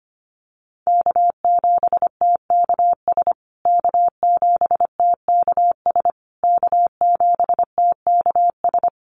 Morse code, heard as a keyed tone, about 25 wpm, 700 Hz. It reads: K7TKS K7TKS K7TKS
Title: USA Call Signs Repeated 3x 25wpm